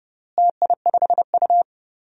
Morse code, heard as a keyed tone, about 30 wpm, 700 Hz. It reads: TI5U
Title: DXCC-100 Call Signs - 4 Characters - Repeated 3x - 30wpm